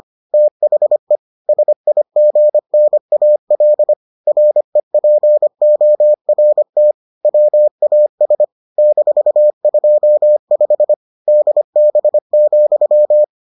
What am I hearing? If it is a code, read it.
Morse code, heard as a keyed tone, about 25 wpm, 600 Hz. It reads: THE SIGNAL REPORT WAS -25 DB,